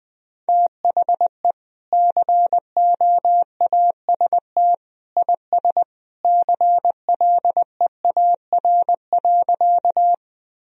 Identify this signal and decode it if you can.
Morse code, heard as a keyed tone, about 19 wpm, 700 Hz. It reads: THE COAST IS CLEAR.
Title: Sentences from Top 1000 Words - Review Part 4 - 20wpm